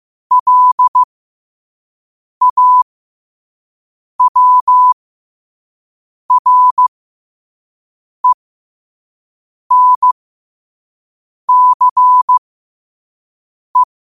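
Morse code, heard as a keyed tone, about 15 wpm, spaced as for 5 wpm, 1000 Hz. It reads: LAWRENCE
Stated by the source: Morse sent at 5 wpm.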